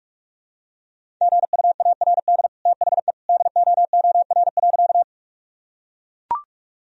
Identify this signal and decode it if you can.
Morse code, heard as a keyed tone, about 41 wpm, 700 Hz. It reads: GUARD THE DOOR.
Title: Sentences from Top 1000 Words - Review Part 4 - 45wpm